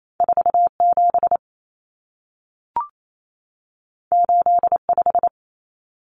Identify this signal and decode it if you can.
Morse code, heard as a keyed tone, about 28 wpm, 700 Hz. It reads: 47 85
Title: Numbers - 2-Digit 28wpm